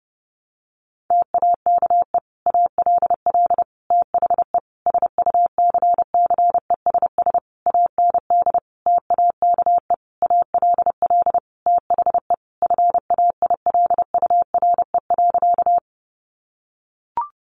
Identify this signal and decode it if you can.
Morse code, heard as a keyed tone, about 30 wpm, 700 Hz. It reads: TAKE ALL THE SUCCESS AND TAKE ALL THE FAILURE.
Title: Sentences from Top 900 Words 30wpm